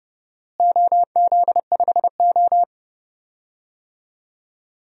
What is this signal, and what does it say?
Morse code, heard as a keyed tone, about 32 wpm, 700 Hz. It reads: OZ5O
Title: DXCC-100 Call Signs - 4 Characters - Repeated 3x - 30wpm